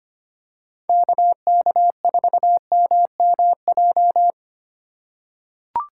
Morse code, heard as a keyed tone, about 25 wpm, 690 Hz. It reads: KK4MMJ
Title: USA Call Signs - 4x Character Spacing - 25wpm